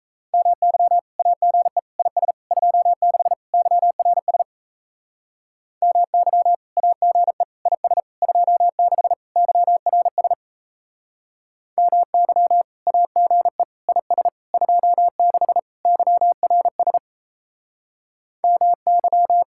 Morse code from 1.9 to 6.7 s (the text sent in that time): IS 26 YRS MY